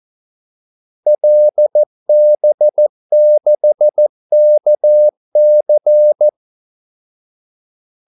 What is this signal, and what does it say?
Morse code, heard as a keyed tone, about 14 wpm, 600 Hz. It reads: LB6KC
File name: Callsigns-14wpm